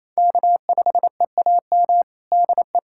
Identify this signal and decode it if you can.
Morse code, heard as a keyed tone, about 28 wpm, 690 Hz. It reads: K5EAM DE
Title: USA Callsign DE Callsign - 5 Characters - 28wpm